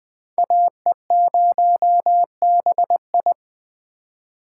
Morse code, heard as a keyed tone, about 20 wpm, 700 Hz. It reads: AE0BI
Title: Speed Racing - USA 2x2 Call Signs 17wpm